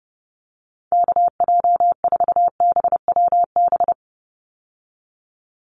Morse code, heard as a keyed tone, about 30 wpm, 700 Hz. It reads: KJ4BWB
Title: USA Call Signs 30wpm